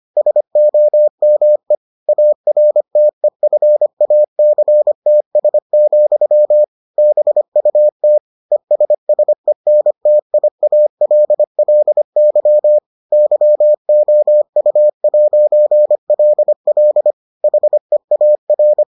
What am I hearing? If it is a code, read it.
Morse code, heard as a keyed tone, about 25 wpm, 600 Hz. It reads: SOME ARTEFACTS, BUT ESSENTIALLY YOU'LL HEAR